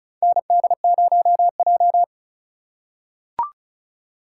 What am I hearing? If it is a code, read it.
Morse code, heard as a keyed tone, about 34 wpm, 700 Hz. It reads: ND0J
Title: USA Call Signs - 3x Character Spacing - 35wpm